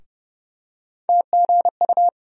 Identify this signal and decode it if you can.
Morse code, heard as a keyed tone, about 30 wpm, 700 Hz. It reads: TGU
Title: USA Call Sign Postfixes - 3 Letters 30wpm